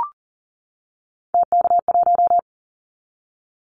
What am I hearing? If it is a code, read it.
Morse code, heard as a keyed tone, about 40 wpm, 700 Hz. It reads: TK1
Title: Callsign Prefixes for Top 100 DXCC - 40wpm